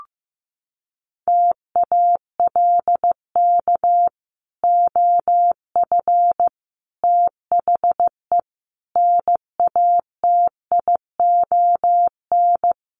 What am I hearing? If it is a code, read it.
Morse code, heard as a keyed tone, about 15 wpm, 700 Hz. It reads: TALK OF THE NATION
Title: Sets of 5 Words 15wpm